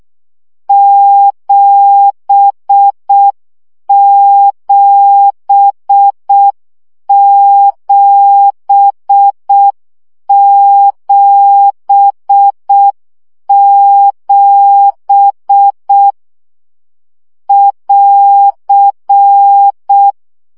Morse code, heard as a keyed tone, about 6 wpm, 800 Hz. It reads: 77777 <AR>
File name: CW gyakorlas 7 5 baud